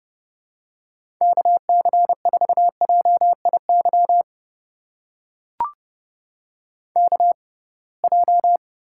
Morse code, heard as a keyed tone, about 30 wpm, spaced as for 23 wpm, 700 Hz. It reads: KC4JIY KJ
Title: USA Call Signs - 6 Characters - 6x Character Spacing - 30wpm